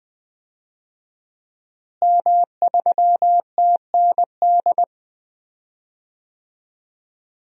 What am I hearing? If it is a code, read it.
Morse code, heard as a keyed tone, about 21 wpm, 700 Hz. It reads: M3TND
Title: DXCC-100 Call Signs - 5 Characters - Repeated 2x - 20wpm